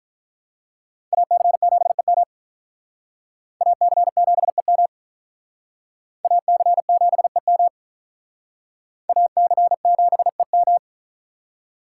Morse code, heard as a keyed tone, about 44 wpm, 700 Hz. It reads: AC7EM AC7EM AC7EM AC7EM